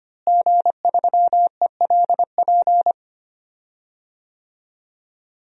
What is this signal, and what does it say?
Morse code, heard as a keyed tone, about 25 wpm, 700 Hz. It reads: G3ELP
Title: DXCC-100 Call Signs - 5 Characters - Repeated 3x - 25wpm